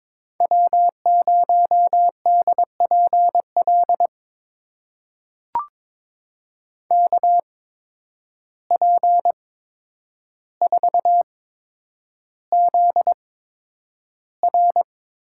Morse code, heard as a keyed tone, about 21 wpm, spaced as for 9 wpm, 700 Hz. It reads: W0DPL KP4ZR
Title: USA Call Signs - 5 Characters - 8x Character Spacing - 22wpm